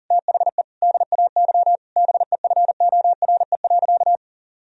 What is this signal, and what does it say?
Morse code, heard as a keyed tone, about 40 wpm, 700 Hz. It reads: THE DAY BEFORE.
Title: Sentences from Top 1000 Words - Review Part 1 - 40wpm